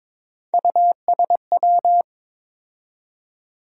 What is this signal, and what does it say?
Morse code, heard as a keyed tone, about 22 wpm, 700 Hz. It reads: USW
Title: USA Call Sign Postfixes - 3 Letters 22wpm